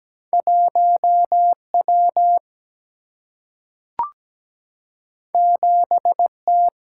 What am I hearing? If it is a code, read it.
Morse code, heard as a keyed tone, about 17 wpm, 700 Hz. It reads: 1W 7T
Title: Single-Letter Single-Number 17wpm